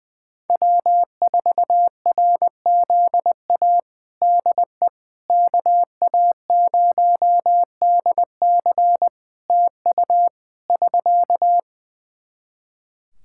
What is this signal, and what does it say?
Morse code, heard as a keyed tone, about 21 wpm, 700 Hz. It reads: W4RZA DE KA0DC TU <SK>
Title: Practice QSO - Exchange 4 - Ending - 20wpm